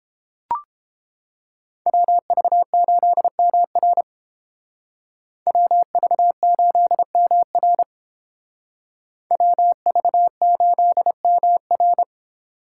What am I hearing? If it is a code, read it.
Morse code, heard as a keyed tone, about 31 wpm, 700 Hz. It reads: WV8MR WV8MR WV8MR